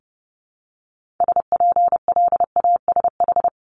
Morse code, heard as a keyed tone, about 30 wpm, 700 Hz. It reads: SPLASH
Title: Multi-letter Phonemes 30wpm